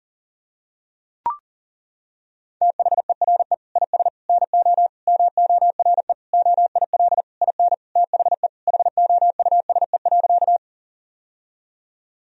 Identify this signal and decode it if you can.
Morse code, heard as a keyed tone, about 40 wpm, 700 Hz. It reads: THERE IS NO MORE OIL IN THE HOUSE.
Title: Sentences from Top 1000 Words - Review Part 4 - 40wpm